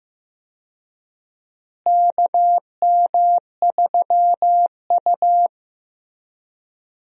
Morse code, heard as a keyed tone, about 15 wpm, 700 Hz. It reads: KM3U
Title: USA Call Signs - 4 Characters - Repeated 2x - 15wpm